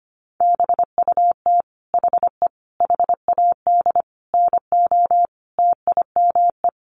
Morse code, heard as a keyed tone, about 25 wpm, 700 Hz. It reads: BUT HE HAD NO TIME
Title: Sentences from Top 1200 Words 25wpm